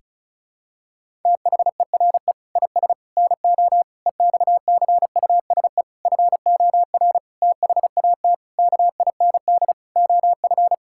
Morse code, heard as a keyed tone, about 36 wpm, 700 Hz. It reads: THERE IS NO EXCUSE FOR THAT KIND OF
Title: Sentences from Top 1000 Words - Review Part 6 - 35wpm